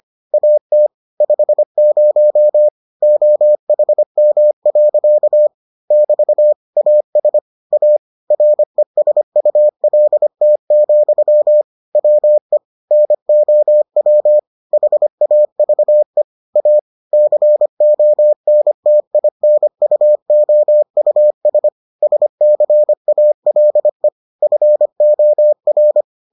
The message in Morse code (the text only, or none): AT 50 OHM. <BT> AS A RESULT, WE NOW HAVE A CONTINUOUS SCALE FOR